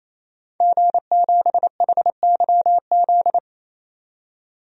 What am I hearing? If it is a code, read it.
Morse code, heard as a keyed tone, about 28 wpm, 700 Hz. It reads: G7HYZ
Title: DXCC-100 Call Signs - 5 Characters - 28wpm